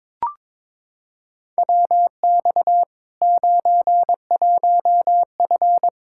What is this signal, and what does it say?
Morse code, heard as a keyed tone, about 22 wpm, 700 Hz. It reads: WX 91F
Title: K1USN SST QSO Practice 22wpm